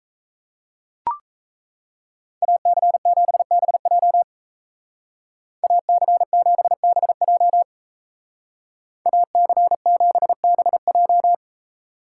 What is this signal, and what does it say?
Morse code, heard as a keyed tone, about 33 wpm, 700 Hz. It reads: AC7BJ AC7BJ AC7BJ
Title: Speed Racing - USA Call Signs 28wpm